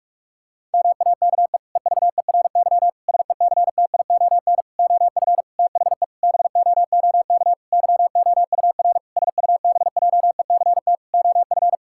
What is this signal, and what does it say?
Morse code, heard as a keyed tone, about 41 wpm, 700 Hz. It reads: MAKE EVERY SECTION OF THE BOOK YOUR SUBJECT OF
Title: Sentences from Top 1000 Words - Review Part 5 - 45wpm